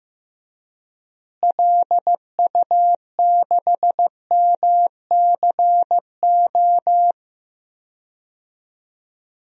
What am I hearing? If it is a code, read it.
Morse code, heard as a keyed tone, about 15 wpm, 700 Hz. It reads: LU6MCO
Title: DXCC-100 Call Signs - 6 Characters - Repeated 2x - 15wpm